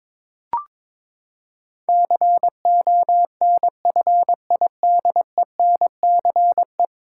Morse code, heard as a keyed tone, about 22 wpm, 700 Hz. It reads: CONFIDENCE
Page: morse-code-ninja-practice-22wpm